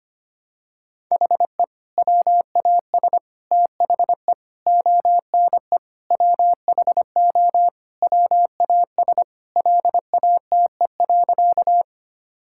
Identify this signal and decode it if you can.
Morse code, heard as a keyed tone, about 25 wpm, 700 Hz. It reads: HE WAS THE ONE WHO WAS LATE.
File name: practice / 25wpm / Sentences with 2-Word Phrases - 25wpm